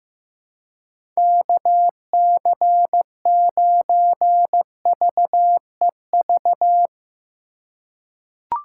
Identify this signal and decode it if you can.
Morse code, heard as a keyed tone, about 15 wpm, 700 Hz. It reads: KC9VEV
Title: USA Call Signs - 6 Characters - 3x Character Spacing - 15wpm